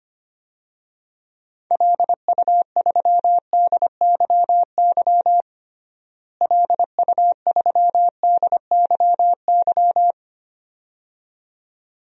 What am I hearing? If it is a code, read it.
Morse code, heard as a keyed tone, about 25 wpm, 700 Hz. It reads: LU3DYY LU3DYY
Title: DXCC-100 Call Signs - 6 Characters - Repeated 2x - 25wpm